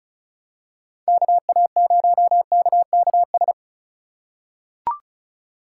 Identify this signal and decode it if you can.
Morse code, heard as a keyed tone, about 34 wpm, 700 Hz. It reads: KA0KKS
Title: USA Call Signs - 4x Character Spacing - 35wpm